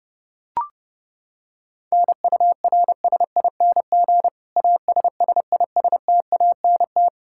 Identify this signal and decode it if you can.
Morse code, heard as a keyed tone, about 32 wpm, 700 Hz. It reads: NURSING ASSISTANT
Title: Practice QSO - Exchange 3 - More Info - 30wpm